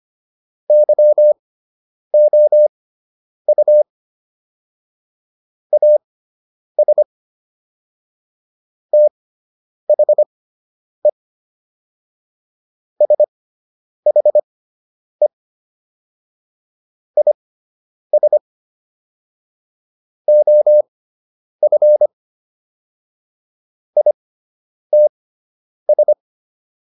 Morse code, heard as a keyed tone, about 25 wpm, spaced as for 9 wpm, 600 Hz. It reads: YOU AS THE SHE IS OF ITS